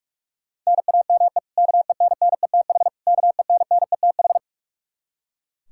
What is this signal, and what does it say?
Morse code, heard as a keyed tone, about 41 wpm, 700 Hz. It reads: NAME KENNETH KENNETH
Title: Practice QSO - Exchange 1 - Beginning - 45wpm